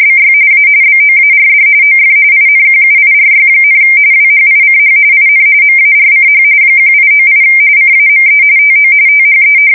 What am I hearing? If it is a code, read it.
RTTY, about 45.45 baud. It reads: BY A BOARD OF INQUIRY, WHICH IN MARCH 1898 CAME TO THE
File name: fsk170 / usnavy3